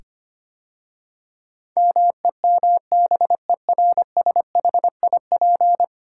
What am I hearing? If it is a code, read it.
Morse code, heard as a keyed tone, about 25 wpm, 690 Hz. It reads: MEMBERSHIP
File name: 10 Letter Words - First 3 Letters Given - 25wpm